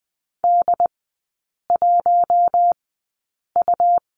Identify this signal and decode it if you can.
Morse code, heard as a keyed tone, about 20 wpm, spaced as for 8 wpm, 700 Hz. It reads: D1U
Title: 100 Lesson 17e - Letter u - Three Characters 20wpm